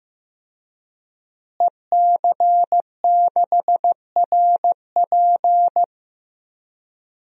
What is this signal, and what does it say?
Morse code, heard as a keyed tone, about 15 wpm, 700 Hz. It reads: EC6RP
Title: DXCC-100 Call Signs - 5 Characters - Repeated 3x - 15wpm